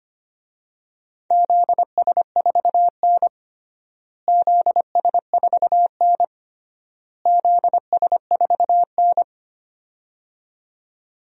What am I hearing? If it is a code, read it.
Morse code, heard as a keyed tone, about 25 wpm, 700 Hz. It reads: ZS4N ZS4N ZS4N
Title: DXCC-100 Call Signs - 4 Characters - Repeated 3x - 25wpm